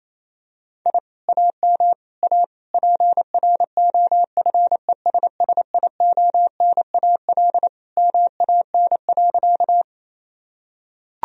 Morse code, heard as a keyed tone, about 28 wpm, 700 Hz. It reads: I AM A PROFESSIONAL MAN.
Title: Sentences from Top 1000 Words - Review Part 4 - 28wpm